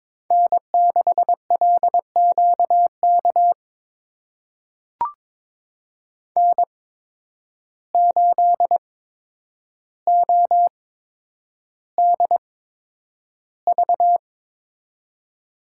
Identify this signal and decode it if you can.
Morse code, heard as a keyed tone, about 22 wpm, spaced as for 9 wpm, 700 Hz. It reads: N6LQK N8ODV
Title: USA Call Signs - 8x Character Spacing - 22wpm